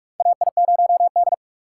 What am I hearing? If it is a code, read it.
Morse code, heard as a keyed tone, about 47 wpm, 700 Hz. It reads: AI0D
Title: USA Call Signs - 4 Characters - 4x Character Spacing - 45wpm